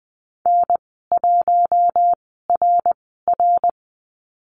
Morse code, heard as a keyed tone, about 19 wpm, 700 Hz. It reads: N 1 R R
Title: 082 Lesson 14g - Number 2 - Call Signs 20wpm